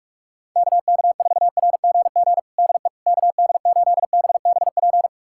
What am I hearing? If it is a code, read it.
Morse code, heard as a keyed tone, about 46 wpm, 700 Hz. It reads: KK4RGG DE KD8BBP
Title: USA Callsign DE Callsign 45wpm